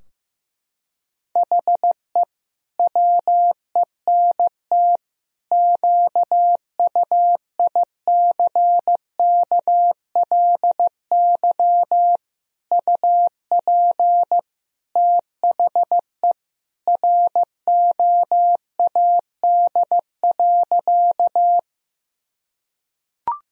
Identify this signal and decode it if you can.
Morse code, heard as a keyed tone, about 16 wpm, 700 Hz. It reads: HE WENT QUICKLY UP THE ROAD.
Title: Sentences from Top 1000 Words - Review Part 2 - 15wpm